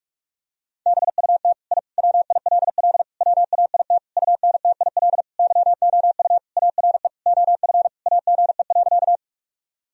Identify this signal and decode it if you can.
Morse code, heard as a keyed tone, about 41 wpm, 700 Hz. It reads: BUT I WILL WAIT UNTIL YOU ARE OF AGE.
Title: Sentences from Top 1000 Words - Review Part 3 - 45wpm